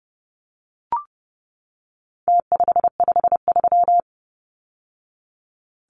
Morse code, heard as a keyed tone, about 30 wpm, 700 Hz. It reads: T553
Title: Numbers - Contest Style - 4-Digit 30wpm